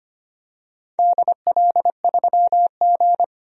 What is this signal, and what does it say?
Morse code, heard as a keyed tone, about 25 wpm, 700 Hz. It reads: DL3G
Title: DXCC-100 Call Signs - 4 Characters - 25wpm